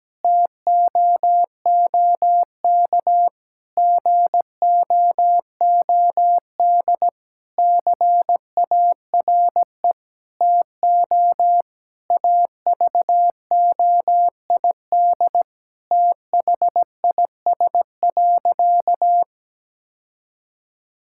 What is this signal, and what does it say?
Morse code, heard as a keyed tone, about 17 wpm, 700 Hz. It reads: TOOK GOOD CARE TO AVOID THIS.
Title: Sentences from Top 1000 Words - Review Part 6 - 17wpm